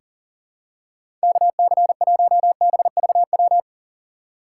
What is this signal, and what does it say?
Morse code, heard as a keyed tone, about 40 wpm, 700 Hz. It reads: KC1BVW
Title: USA Call Signs - 6 Characters - 2x Character Spacing - 40wpm